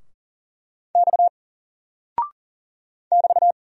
Morse code, heard as a keyed tone, about 40 wpm, 700 Hz. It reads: X <BT>
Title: Single Character (Letter:Number:Punctuation:Prosign) Alt 40wpm